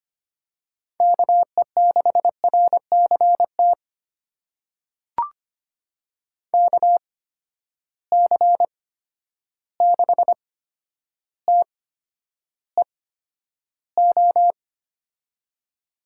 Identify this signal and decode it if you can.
Morse code, heard as a keyed tone, about 25 wpm, spaced as for 11 wpm, 700 Hz. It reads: KE6RCT KC6TEO
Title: USA Call Signs - 6 Characters - 8x Character Spacing - 25wpm